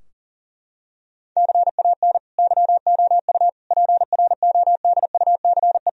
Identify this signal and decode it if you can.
Morse code, heard as a keyed tone, about 40 wpm, 700 Hz. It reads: CAN YOU PRODUCE
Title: Sentences from Top 1000 Words - Review Part 3 - 40wpm